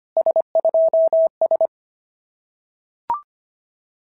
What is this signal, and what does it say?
Morse code, heard as a keyed tone, about 24 wpm, 650 Hz. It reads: S2S
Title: SOTA QSO Practice 25wpm